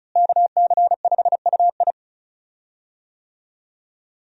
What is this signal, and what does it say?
Morse code, heard as a keyed tone, about 36 wpm, 700 Hz. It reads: KC5UI
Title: USA Call Signs - 5 Characters - Repeated 3x - 35wpm